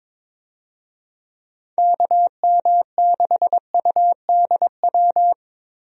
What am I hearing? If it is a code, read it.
Morse code, heard as a keyed tone, about 22 wpm, 700 Hz. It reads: KM6UDW
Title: USA Call Signs - 6 Characters - Repeated 3x - 22wpm